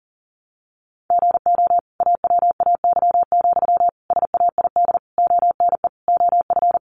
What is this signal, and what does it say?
Morse code, heard as a keyed tone, about 40 wpm, 700 Hz. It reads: GO AWAY, SAID ONE OF
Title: Sentences from Top 1500 Words 40wpm